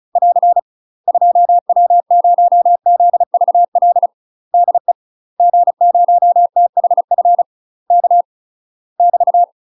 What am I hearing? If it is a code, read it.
Morse code, heard as a keyed tone, about 36 wpm, 700 Hz. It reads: <AR> 2W0ZVL DE G0THF K <BT>
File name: UK MASTER QSOs TRAINING / 35WPM / UK MASTER QSOs 0004